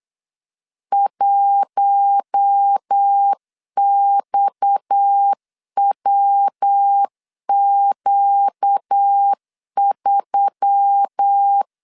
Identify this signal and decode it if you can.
Morse code, heard as a keyed tone, about 8 wpm, 800 Hz. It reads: 1XWQ3
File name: LEZIONI / 6 PKX0 / 6 PKX0 8wpm 3341